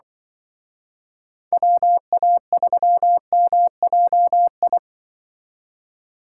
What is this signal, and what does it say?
Morse code, heard as a keyed tone, about 24 wpm, 700 Hz. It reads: WA3MJI
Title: Speed Racing - USA 2x3 Call Signs 20wpm